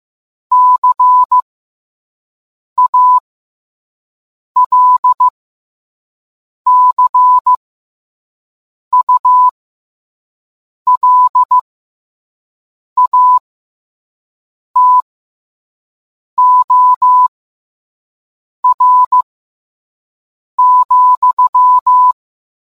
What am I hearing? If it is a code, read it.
Morse code, heard as a keyed tone, about 15 wpm, spaced as for 5 wpm, 1000 Hz. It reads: CALCULATOR,